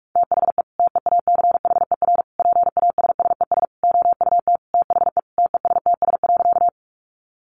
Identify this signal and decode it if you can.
Morse code, heard as a keyed tone, about 44 wpm, 700 Hz. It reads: THE TEACHER PASSES OUT THE TESTS.
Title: Additional Easy Phrases 45wpm